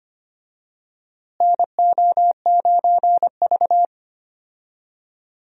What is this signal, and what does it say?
Morse code, heard as a keyed tone, about 25 wpm, 700 Hz. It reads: NO9V
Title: USA Call Signs - 4 Characters - Repeated 2x - 25wpm